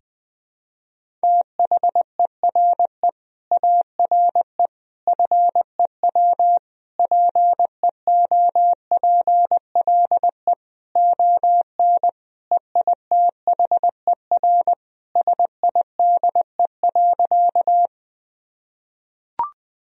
Morse code, heard as a keyed tone, about 20 wpm, 700 Hz. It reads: THERE ARE FEW PEOPLE ON EITHER SIDE.
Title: Sentences from Top 1000 Words - Review Part 4 - 20wpm